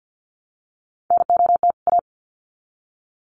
Morse code, heard as a keyed tone, about 50 wpm, 700 Hz. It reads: NOT A
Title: Sets of 2 Words 50wpm